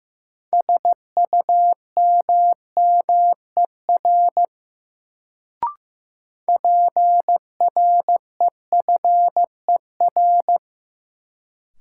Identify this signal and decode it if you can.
Morse code, heard as a keyed tone, about 15 wpm, 700 Hz. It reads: SUMMER PREFER
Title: Sentences from Top 1100 Words - New Word Drill Alt - 15wpm